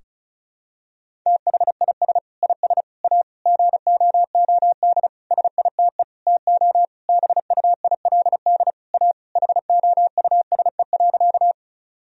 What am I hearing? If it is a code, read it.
Morse code, heard as a keyed tone, about 36 wpm, 700 Hz. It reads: THIS IS A GOOD SITE TO BUILD A HOUSE.
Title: Sentences from Top 1000 Words - Review Part 4 - 35wpm